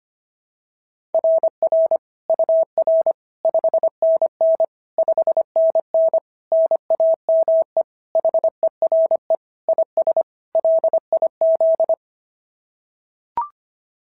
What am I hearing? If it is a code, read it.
Morse code, heard as a keyed tone, about 24 wpm, 650 Hz. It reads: RR UR 5NN 5NN NAME HERE IS LIZ
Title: SOTA QSO Practice 25wpm